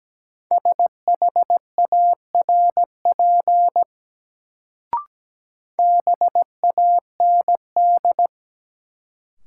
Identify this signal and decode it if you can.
Morse code, heard as a keyed tone, about 17 wpm, 700 Hz. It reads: SHARP BAND
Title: Sentences from Top 1600 Words - New Word Drill - 17wpm